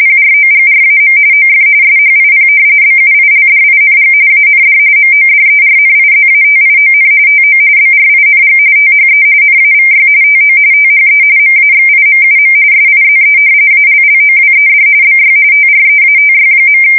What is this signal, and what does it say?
RTTY, about 45.45 baud. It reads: PROGRAM OF SHIP BUILDING AND MODERNIZATION TO BRING THE NAVY TO THE MAXIMUM SIZE ALLOWED BY TREATY.
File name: fsk170 / usnavy4